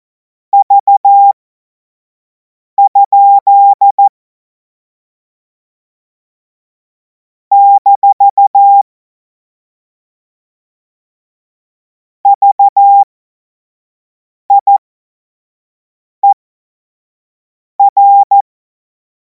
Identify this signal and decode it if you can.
Morse code, heard as a keyed tone, about 14 wpm, spaced as for 5 wpm, 800 Hz. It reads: V? - VIER
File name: Morsekurs / Witze / Witz 07. April 2026 25 F70